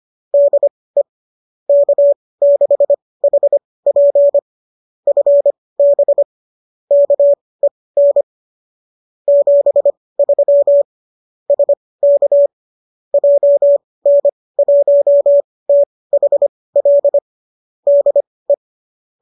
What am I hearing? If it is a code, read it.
Morse code, heard as a keyed tone, about 25 wpm, spaced as for 18 wpm, 570 Hz. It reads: DE K6HP FB KEN 73 SK JN1THL DE